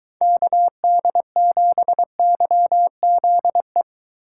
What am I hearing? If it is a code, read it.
Morse code, heard as a keyed tone, about 22 wpm, 700 Hz. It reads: KD7YZE